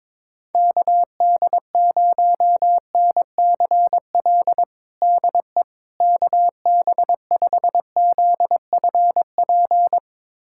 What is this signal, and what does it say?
Morse code, heard as a keyed tone, about 21 wpm, 700 Hz. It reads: KD0NCL DE KB5ZFP
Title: USA Callsign DE Callsign - 6 Characters - 22wpm